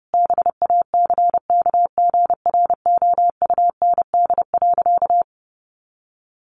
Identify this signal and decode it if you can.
Morse code, heard as a keyed tone, about 30 wpm, 700 Hz. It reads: BACKGROUND.
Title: Sentences from Top 1500 Words 30wpm